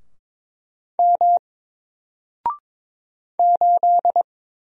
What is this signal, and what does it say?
Morse code, heard as a keyed tone, about 22 wpm, 700 Hz. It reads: M 8
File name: Single-Letter-Number-22wpm